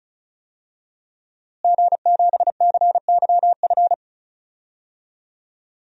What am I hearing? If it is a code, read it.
Morse code, heard as a keyed tone, about 34 wpm, 700 Hz. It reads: G7CYF
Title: DXCC-100 Call Signs - 5 Characters - Repeated 2x - 35wpm